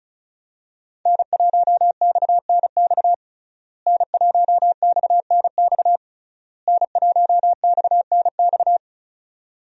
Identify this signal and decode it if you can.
Morse code, heard as a keyed tone, about 36 wpm, 700 Hz. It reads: N1XNX N1XNX N1XNX
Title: USA Call Signs - 5 Characters - Repeated 3x - 35wpm